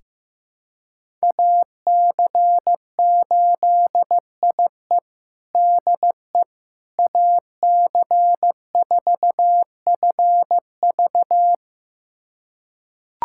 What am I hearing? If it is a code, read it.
Morse code, heard as a keyed tone, about 15 wpm, 700 Hz. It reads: AC8IE DE AC4FV
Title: USA Callsign DE Callsign - 5 Characters - 15wpm